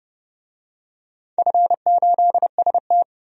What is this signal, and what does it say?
Morse code, heard as a keyed tone, about 30 wpm, 700 Hz. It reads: F8ST
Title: DXCC-100 Call Signs - 4 Characters - 30wpm